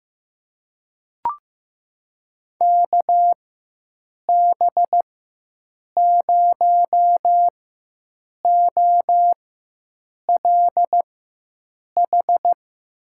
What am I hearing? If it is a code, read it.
Morse code, heard as a keyed tone, about 15 wpm, spaced as for 7 wpm, 700 Hz. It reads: KB0OLH
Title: USA Call Signs - 6 Characters - 4x Character Spacing - 15wpm